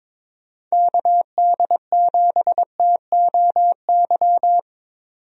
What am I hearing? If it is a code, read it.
Morse code, heard as a keyed tone, about 22 wpm, 700 Hz. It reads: KD7TOY
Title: USA Call Signs - 6 Characters - 22wpm